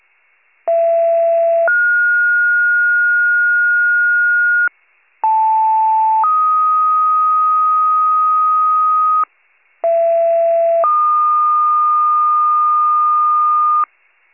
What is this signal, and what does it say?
Two-tone page, heard as a one-second tone, then a three-second tone, about 670 then 1430 Hz, about 880 then 1250 Hz, about 670 then 1220 Hz. It is heard from 0.7 to 13.8 s.